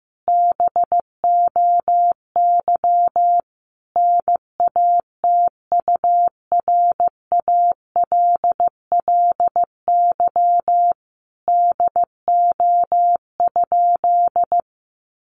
Morse code, heard as a keyed tone, about 15 wpm, 700 Hz. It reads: BOY NATURALLY DO?
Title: Sentences from Top 1300 Words 15wpm